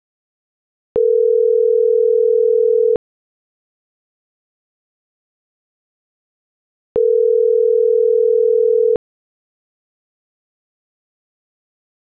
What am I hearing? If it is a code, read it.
Ringback tone (North America), 440 + 480 Hz.